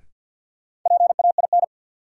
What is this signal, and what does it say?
Morse code, heard as a keyed tone, about 52 wpm, 700 Hz. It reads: PAIN
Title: Sentences from Top 1600 Words - New Word Drill Alt - 50wpm